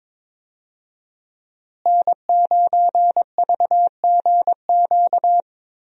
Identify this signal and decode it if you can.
Morse code, heard as a keyed tone, about 22 wpm, 700 Hz. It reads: N9VGQ
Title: USA Call Signs - 5 Characters - Repeated 2x - 22wpm